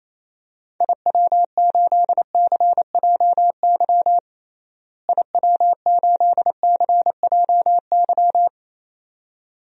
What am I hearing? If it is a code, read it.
Morse code, heard as a keyed tone, about 28 wpm, 700 Hz. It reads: IW8CJY IW8CJY
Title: DXCC-100 Call Signs - 6 Characters - Repeated 2x - 28wpm